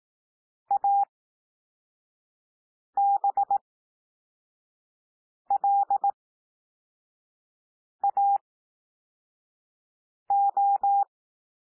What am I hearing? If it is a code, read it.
Morse code, heard as a keyed tone, about 18 wpm, spaced as for 4 wpm, 800 Hz. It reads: ABLAO
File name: cwmp3 / L2 SOUND LETTERS 18 wpm